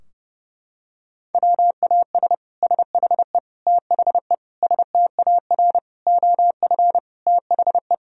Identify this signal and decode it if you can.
Morse code, heard as a keyed tone, about 30 wpm, 700 Hz. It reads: WAS SHE THE STAR OF THE
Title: Sentences from Top 1000 Words - Review Part 3 - 30wpm